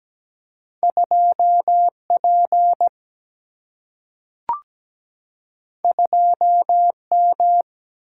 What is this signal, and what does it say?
Morse code, heard as a keyed tone, about 17 wpm, 700 Hz. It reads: 2P 2M
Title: Single-Letter Single-Number 17wpm